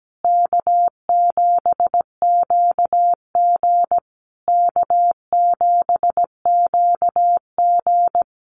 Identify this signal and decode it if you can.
Morse code, heard as a keyed tone, about 17 wpm, 700 Hz. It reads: K7QG K7QG
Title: USA Call Signs Repeated 3x 17wpm